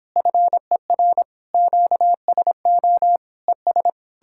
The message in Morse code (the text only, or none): FER QSO ES